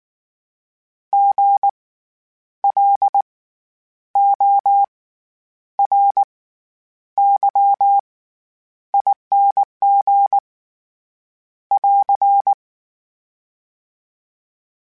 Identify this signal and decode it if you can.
Morse code, heard as a keyed tone, about 19 wpm, spaced as for 9 wpm, 800 Hz. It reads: GLORYING <AR>
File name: Special INGs / Special INGs-mp3s / SlowToFaster ing @18